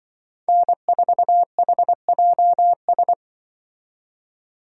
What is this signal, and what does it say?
Morse code, heard as a keyed tone, about 25 wpm, 700 Hz. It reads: N4HJS
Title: Speed Racing - USA 1x3 Call Signs 20wpm